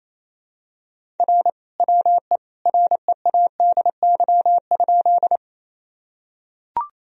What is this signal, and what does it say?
Morse code, heard as a keyed tone, about 27 wpm, 700 Hz. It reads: R WE READY?
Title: Sentences from Top 400 Words and Abbreviations 28wpm